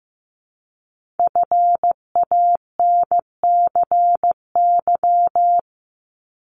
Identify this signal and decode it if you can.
Morse code, heard as a keyed tone, about 15 wpm, 700 Hz. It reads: FANCY
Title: Multi-letter Phonemes Repeated 3x 15wpm